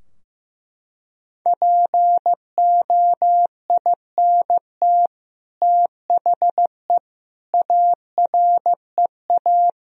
Morse code, heard as a keyed tone, about 15 wpm, 700 Hz. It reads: POINT THE AREA
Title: Sentences from Top 1000 Words - Review Part 2 - 15wpm